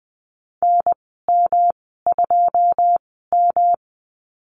Morse code, heard as a keyed tone, about 20 wpm, spaced as for 14 wpm, 700 Hz. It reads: NM2M
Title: 128 Lesson 21g - Number 6 - Call Signs 20wpm